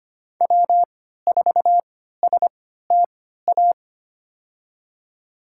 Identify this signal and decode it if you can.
Morse code, heard as a keyed tone, about 25 wpm, spaced as for 14 wpm, 700 Hz. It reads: W4STA
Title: USA Call Signs - 5 Characters - 3x Character Spacing - 25wpm